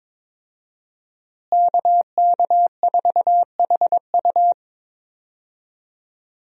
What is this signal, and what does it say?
Morse code, heard as a keyed tone, about 21 wpm, 700 Hz. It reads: KK4HU
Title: USA Call Signs - 5 Characters - Repeated 3x - 22wpm